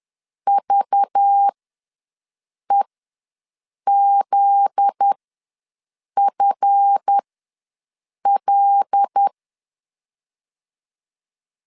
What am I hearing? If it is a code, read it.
Morse code, heard as a keyed tone, about 10 wpm, spaced as for 6 wpm, 800 Hz. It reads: VEZFL
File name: LEZIONI / 7 GF9Z / 7 GF9Z ws 10wpm 3919